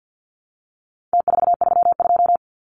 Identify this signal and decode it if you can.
Morse code, heard as a keyed tone, about 50 wpm, 700 Hz. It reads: T432
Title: Numbers - Contest Style - 4-Digit 50wpm